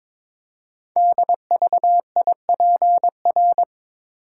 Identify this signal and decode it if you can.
Morse code, heard as a keyed tone, about 22 wpm, 700 Hz. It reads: DVIPR
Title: Code Groups - 5 Letters - 22wpm